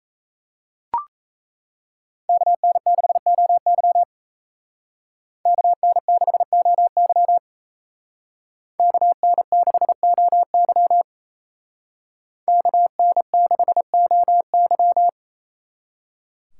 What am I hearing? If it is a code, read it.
Morse code, heard as a keyed tone, about 34 wpm, 700 Hz. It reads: KN6OY KN6OY KN6OY KN6OY
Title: Speed Racing - USA 2x2 Call Signs 28wpm